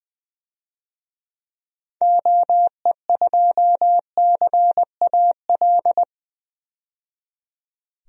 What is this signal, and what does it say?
Morse code, heard as a keyed tone, about 21 wpm, 700 Hz. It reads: OE2CAL
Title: DXCC-100 Call Signs - 6 Characters - 20wpm